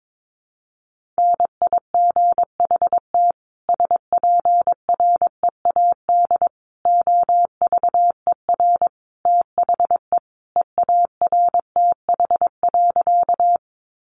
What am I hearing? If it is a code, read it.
Morse code, heard as a keyed tone, about 22 wpm, 700 Hz. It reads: NIGHT SPREAD OVER THE EARTH.
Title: Sentences from Top 1200 Words 22wpm